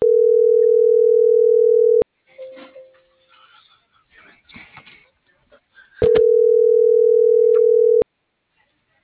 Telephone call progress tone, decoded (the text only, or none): ringback tone